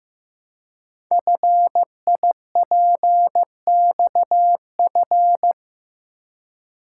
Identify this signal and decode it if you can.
Morse code, heard as a keyed tone, about 15 wpm, 700 Hz. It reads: FIPXF
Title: Code Groups - 5 Letters - 15wpm